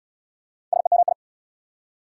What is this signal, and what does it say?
Morse code, heard as a keyed tone, about 69 wpm, 690 Hz. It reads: SEMI